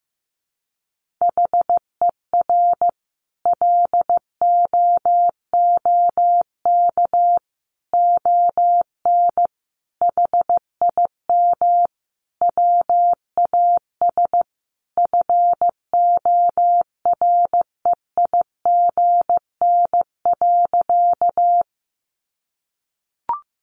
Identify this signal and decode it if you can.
Morse code, heard as a keyed tone, about 14 wpm, 700 Hz. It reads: HER LOOK ON HIM WAS FOREIGN.
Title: Sentences from Top 1400 Words 15wpm